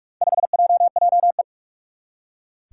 Morse code, heard as a keyed tone, about 46 wpm, 700 Hz. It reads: 5JJE
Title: Four Challenging Letter-Number Combinations 45wpm